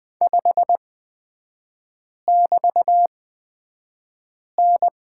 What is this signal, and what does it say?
Morse code, heard as a keyed tone, about 20 wpm, spaced as for 5 wpm, 690 Hz. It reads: 5<BT>N
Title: 229 Lesson 37e - Letter x - Three Characters 20wpm